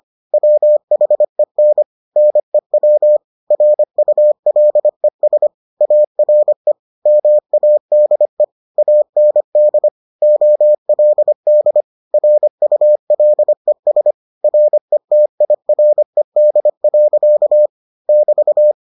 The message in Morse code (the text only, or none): WHEN NEW RULES ARE MADE AND OLD RULES RETIRED. <BT>